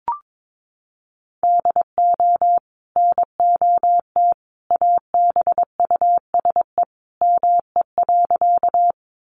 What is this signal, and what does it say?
Morse code, heard as a keyed tone, about 22 wpm, 700 Hz. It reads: DO NOT ABUSE ME.
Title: Sentences from Top 1300 Words 22wpm